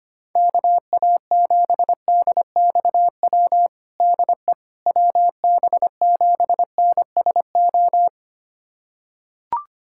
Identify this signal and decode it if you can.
Morse code, heard as a keyed tone, about 25 wpm, 700 Hz. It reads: KA7DXW DE WB7NSO
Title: USA Callsign DE Callsign 25wpm